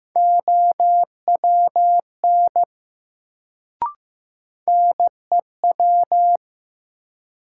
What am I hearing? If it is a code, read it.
Morse code, heard as a keyed tone, about 15 wpm, 700 Hz. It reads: OWN NEW
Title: Sets of 2 Words - New Word Drill - 15wpm